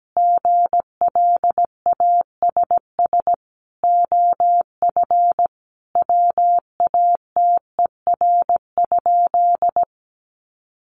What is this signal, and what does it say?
Morse code, heard as a keyed tone, about 17 wpm, 700 Hz. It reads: GLASS OF WATER?
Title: Sentences from Top 1300 Words 17wpm